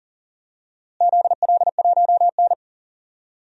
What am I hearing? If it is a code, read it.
Morse code, heard as a keyed tone, about 40 wpm, 700 Hz. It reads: ZL1N
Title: DXCC-100 Call Signs - 4 Characters - Repeated 2x - 40wpm